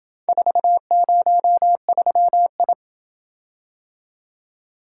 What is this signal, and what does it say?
Morse code, heard as a keyed tone, about 27 wpm, 700 Hz. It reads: 403I